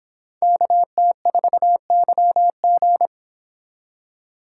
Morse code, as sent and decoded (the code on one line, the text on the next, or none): -.- - ....- -.-- --.
KT4YG